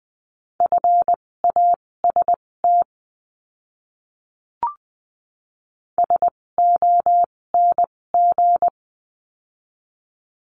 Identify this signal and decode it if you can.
Morse code, heard as a keyed tone, about 20 wpm, spaced as for 16 wpm, 700 Hz. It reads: FAST SONG
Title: 256 Lesson 42a - Words 12-20wpm